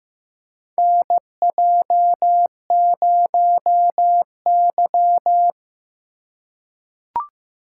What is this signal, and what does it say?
Morse code, heard as a keyed tone, about 15 wpm, 700 Hz. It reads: NJ0Y
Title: USA Call Signs - 4 Characters - 3x Character Spacing - 15wpm